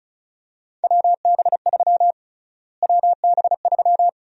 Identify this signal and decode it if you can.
Morse code, heard as a keyed tone, about 36 wpm, 700 Hz. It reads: WB3 WB3
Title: USA Call Sign Prefixes - Repeated 2x - 35wpm